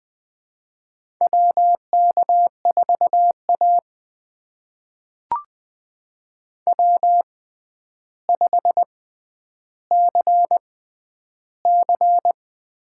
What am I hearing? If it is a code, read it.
Morse code, heard as a keyed tone, about 20 wpm, spaced as for 10 wpm, 700 Hz. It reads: WK4A W5CC
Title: USA Call Signs - 4 Characters - 6x Character Spacing - 20wpm